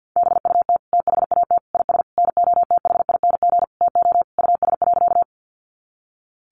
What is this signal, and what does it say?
Morse code, heard as a keyed tone, about 50 wpm, 700 Hz. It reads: BUT THAT IS NOTHING TO US.
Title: Sentences from Top 500 Words 50wpm